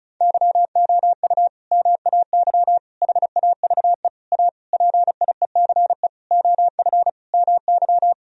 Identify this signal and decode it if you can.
Morse code, heard as a keyed tone, about 34 wpm, 700 Hz. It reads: YOU MAY HAVE A PIECE OF MY
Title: Sentences from Top 1000 Words - Review Part 2 - 35wpm